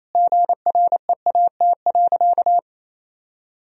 Morse code, heard as a keyed tone, about 28 wpm, 700 Hz. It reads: GREAT.
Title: Sentences from Top 1000 Words - Review Part 4 - 28wpm